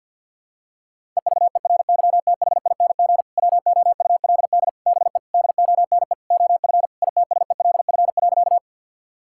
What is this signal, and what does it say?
Morse code, heard as a keyed tone, about 51 wpm, 690 Hz. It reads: EVERYTHING WOULD BE DONE OF ITSELF.
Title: Sentences from Top 1000 Words - Review Part 5 - 50wpm